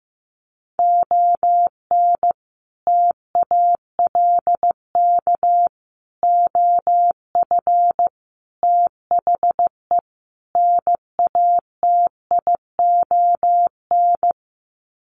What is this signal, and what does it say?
Morse code, heard as a keyed tone, about 15 wpm, 700 Hz. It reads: ON TALK OF THE NATION
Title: Sets of 5 Words 15wpm